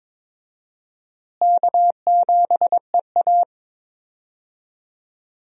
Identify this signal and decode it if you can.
Morse code, heard as a keyed tone, about 22 wpm, 700 Hz. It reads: K7EA
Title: USA Call Signs - 4 Characters - Repeated 3x - 22wpm